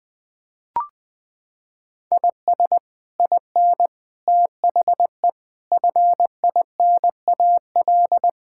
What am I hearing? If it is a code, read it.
Morse code, heard as a keyed tone, about 21 wpm, 700 Hz. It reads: IS IN THE FINAL
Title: Sets of 8 Words - 20wpm